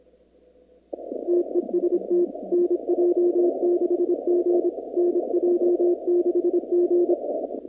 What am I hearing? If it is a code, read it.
Morse code, heard as a keyed tone, about 26 wpm, 350 Hz. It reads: TEST NJ6G NJ6G
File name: contests / 2019-cw-wpx-kt5j-k5pi- -2330 0-25